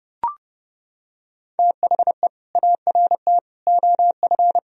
Morse code, heard as a keyed tone, about 30 wpm, 700 Hz. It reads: THE ART OF
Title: Sentences from Top 1000 Words - Review Part 3 - 30wpm